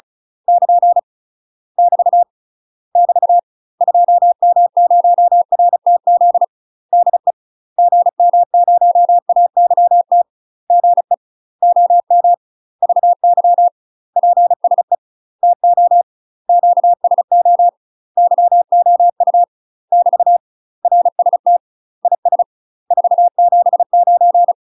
Morse code, heard as a keyed tone, about 36 wpm, 700 Hz. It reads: <KN> <BT> <BT> 2M0RTZ DE GM0AYT GE OM VY PSE TO QSO YOU <BT> RST IS 479
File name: UK MASTER QSOs TRAINING / 35WPM / UK MASTER QSOs 0038